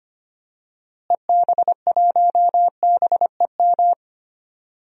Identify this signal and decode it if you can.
Morse code, heard as a keyed tone, about 25 wpm, 700 Hz. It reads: EB1BEM
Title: DXCC-100 Call Signs - 6 Characters - Repeated 3x - 25wpm